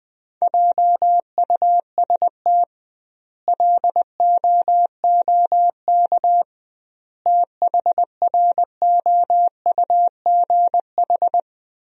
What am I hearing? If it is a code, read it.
Morse code, heard as a keyed tone, about 21 wpm, 700 Hz. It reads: JUST LOOK THROUGH
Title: 2x Word-Spacing - Sentences from Top 200 Words - 20wpm